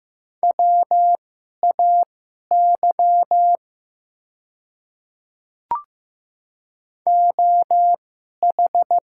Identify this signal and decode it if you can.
Morse code, heard as a keyed tone, about 15 wpm, spaced as for 11 wpm, 700 Hz. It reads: WAY OH
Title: 2x Character Spacing - Top 500 Words - 15wpm